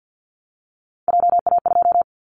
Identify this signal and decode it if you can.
Morse code, heard as a keyed tone, about 50 wpm, 700 Hz. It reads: JA2
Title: Callsign Prefixes for Top 50 DXCC - 50wpm